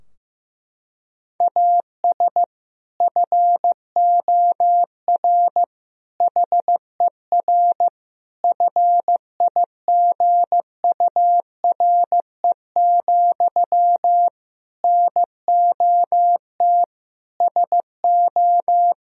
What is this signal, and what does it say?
Morse code, heard as a keyed tone, about 15 wpm, 700 Hz. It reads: AS FOR HER FIGURE, NOT SO
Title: Sentences from Top 1000 Words - Review Part 2 - 15wpm